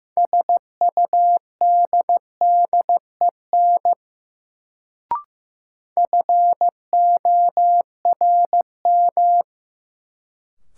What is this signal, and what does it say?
Morse code, heard as a keyed tone, about 15 wpm, 700 Hz. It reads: SUDDEN FORM
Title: Sets of 4 Words - New Word Drill - 15wpm